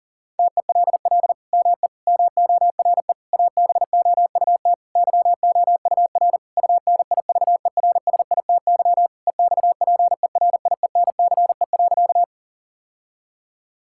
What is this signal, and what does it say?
Morse code, heard as a keyed tone, about 37 wpm, 700 Hz. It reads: TELL ME MORE ABOUT YOUR UNIVERSITY EXPERIENCE.
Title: Sentences from Top 1000 Words - Review Part 5 - 40wpm